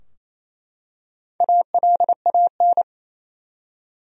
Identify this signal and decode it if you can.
Morse code, heard as a keyed tone, about 28 wpm, 700 Hz. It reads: ALAN
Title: USA Common Names - Max 5 Letters - 28wpm